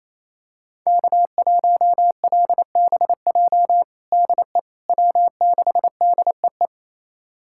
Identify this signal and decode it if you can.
Morse code, heard as a keyed tone, about 28 wpm, 700 Hz. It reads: K1LBJ DE W6DEE
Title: USA Callsign DE Callsign - 5 Characters - 28wpm